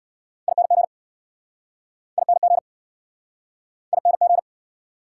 Morse code, heard as a keyed tone, about 67 wpm, 690 Hz. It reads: ING ING ING